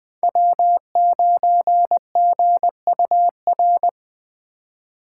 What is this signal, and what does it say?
Morse code, heard as a keyed tone, about 20 wpm, 700 Hz. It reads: W9GUR
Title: Speed Racing - USA 1x3 Call Signs 17wpm